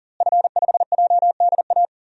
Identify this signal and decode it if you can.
Morse code, heard as a keyed tone, about 40 wpm, 700 Hz. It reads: F5JDA
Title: DXCC-100 Call Signs - 5 Characters - Repeated 3x - 40wpm